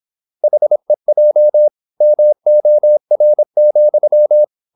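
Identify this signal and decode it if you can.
Morse code, heard as a keyed tone, about 26 wpm, 600 Hz. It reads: HEJ MOR,
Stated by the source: High-speed (QRQ) Morse at 26 wpm.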